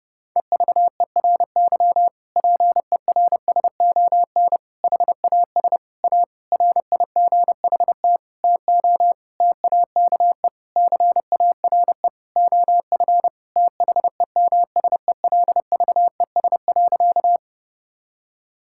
Morse code, heard as a keyed tone, about 32 wpm, 700 Hz. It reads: EVERY PERSON HAS A RIGHT TO TAKE CARE OF THEMSELVES.
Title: Sentences from Top 1000 Words - Review Part 5 - 30wpm